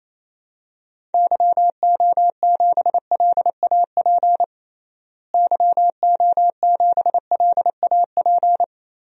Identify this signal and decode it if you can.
Morse code, heard as a keyed tone, about 28 wpm, 700 Hz. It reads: YO7LAP YO7LAP
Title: DXCC-100 Call Signs - 6 Characters - Repeated 3x - 28wpm